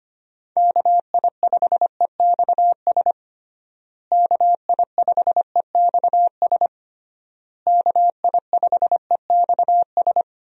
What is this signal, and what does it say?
Morse code, heard as a keyed tone, about 25 wpm, 700 Hz. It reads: KI5EXS KI5EXS KI5EXS
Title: USA Call Signs - 6 Characters - Repeated 3x - 25wpm